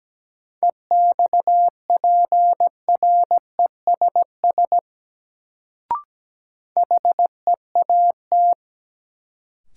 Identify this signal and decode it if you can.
Morse code, heard as a keyed tone, about 17 wpm, 700 Hz. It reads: EXPRESS HEAT
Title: Sentences from Top 1100 Words - New Word Drill Alt - 17wpm